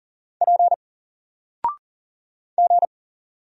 Morse code, heard as a keyed tone, about 40 wpm, 700 Hz. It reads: P G
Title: Single Letter-Number Alt 40wpm